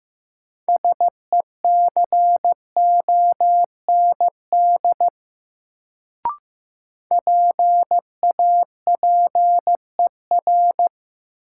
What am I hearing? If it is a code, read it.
Morse code, heard as a keyed tone, about 15 wpm, 700 Hz. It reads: SECOND PAPER
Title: Sentences from Top 300 Words - New Word Drill - 15wpm